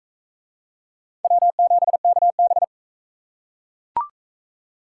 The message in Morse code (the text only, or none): W7KB